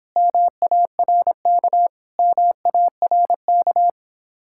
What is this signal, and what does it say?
Morse code, heard as a keyed tone, about 26 wpm, 700 Hz. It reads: MARK MARK